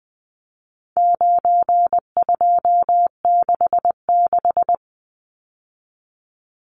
Morse code, heard as a keyed tone, about 20 wpm, 700 Hz. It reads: 9266
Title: Numbers - 4-Digit 20wpm